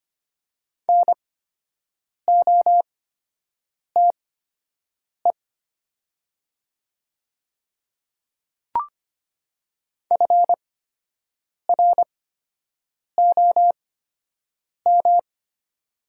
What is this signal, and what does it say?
Morse code, heard as a keyed tone, about 25 wpm, spaced as for 7 wpm, 700 Hz. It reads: NOTE FROM
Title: 8x Character Spacing - Top 500 Words - 25wpm